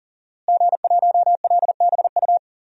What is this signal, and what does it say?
Morse code, heard as a keyed tone, about 40 wpm, 700 Hz. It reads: G1LBU
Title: DXCC-100 Call Signs - 5 Characters - Repeated 2x - 40wpm